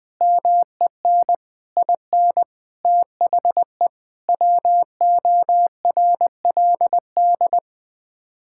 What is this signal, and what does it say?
Morse code, heard as a keyed tone, about 20 wpm, 700 Hz. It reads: MEN IN THE WORLD
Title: Sets of 8 Words - 20wpm